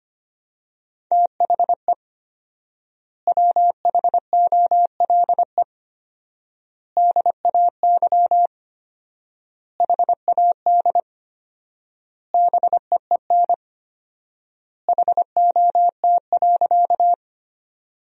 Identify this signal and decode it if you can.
Morse code, heard as a keyed tone, about 25 wpm, 700 Hz. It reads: THE WHOLE DAY HAD BEEN HOT.
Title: 4x Word-Spacing - Sentences from Top 500 Words - 25wpm